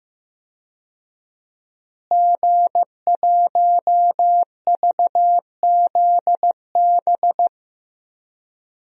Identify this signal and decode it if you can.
Morse code, heard as a keyed tone, about 15 wpm, 700 Hz. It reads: G1VZB
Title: DXCC-100 Call Signs - 5 Characters - Repeated 3x - 15wpm